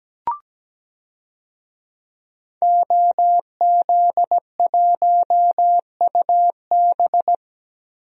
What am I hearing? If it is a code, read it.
Morse code, heard as a keyed tone, about 17 wpm, 700 Hz. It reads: OZ1UB
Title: DXCC-100 Call Signs - 5 Characters - Repeated 2x - 17wpm